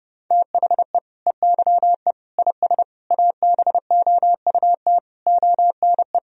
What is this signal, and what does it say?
Morse code, heard as a keyed tone, about 30 wpm, 700 Hz. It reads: THE EYE IS ABOUT ONE
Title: Sentences from Top 1000 Words - Review Part 3 - 30wpm